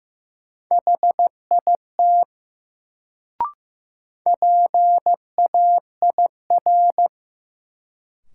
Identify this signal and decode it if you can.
Morse code, heard as a keyed tone, about 15 wpm, 700 Hz. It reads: HIT PAIR
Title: Sentences from Top 1200 Words - New Word Drill - 15wpm